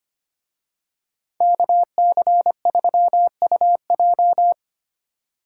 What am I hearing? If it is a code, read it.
Morse code, heard as a keyed tone, about 28 wpm, 700 Hz. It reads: KC3UJ
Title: Speed Racing - USA 2x2 Call Signs 25wpm